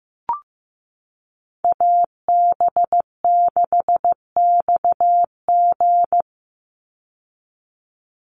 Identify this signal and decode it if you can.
Morse code, heard as a keyed tone, about 15 wpm, 700 Hz. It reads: AB6XG
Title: Call Signs Repeated 2x - 15wpm